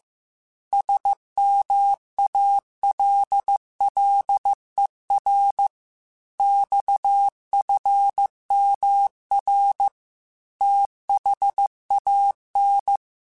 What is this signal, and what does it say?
Morse code, heard as a keyed tone, about 15 wpm, 800 Hz. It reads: SMALLER XFMR THAN
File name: Technical Subjects / Technical Subjects-mp3s / Sw Pwr Sup @13